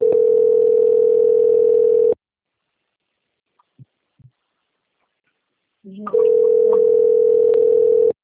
Ringback tone (North America), 440 + 480 Hz.